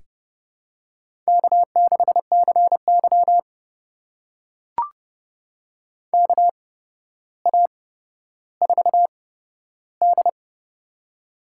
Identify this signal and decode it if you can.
Morse code, heard as a keyed tone, about 31 wpm, spaced as for 13 wpm, 700 Hz. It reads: K6CY KA4D
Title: USA Call Signs - 4 Characters - 8x Character Spacing - 30wpm